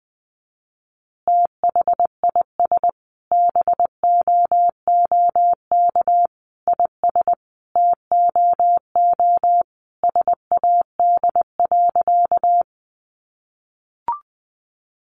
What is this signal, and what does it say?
Morse code, heard as a keyed tone, about 20 wpm, 700 Hz. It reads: THIS BOOK IS TOO SAD.
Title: Additional Easy Phrases 20wpm